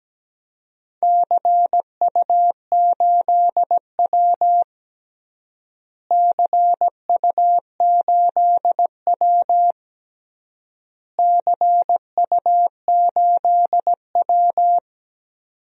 Morse code, heard as a keyed tone, about 17 wpm, 700 Hz. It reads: CU8W CU8W CU8W
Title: DXCC-100 Call Signs - 4 Characters - Repeated 3x - 17wpm